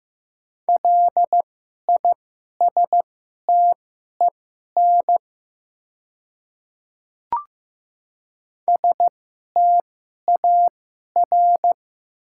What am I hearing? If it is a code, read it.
Morse code, heard as a keyed tone, about 15 wpm, spaced as for 11 wpm, 700 Hz. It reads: LISTEN STAR
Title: 2x Character Spacing - Top 500 Words - 15wpm